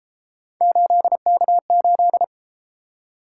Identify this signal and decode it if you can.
Morse code, heard as a keyed tone, about 33 wpm, 700 Hz. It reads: 8K8